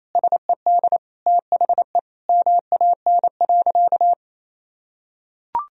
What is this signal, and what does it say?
Morse code, heard as a keyed tone, about 28 wpm, 700 Hz. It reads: SED THE MAN.
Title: Sentences from Top 200 Words and Abbreviations 28wpm